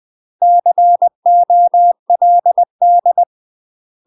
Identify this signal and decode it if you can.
Morse code, heard as a keyed tone, about 20 wpm, 700 Hz. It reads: COLD
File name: UK MASTER QSOs TRAINING / 20WPM / UK MASTER QSOs 0035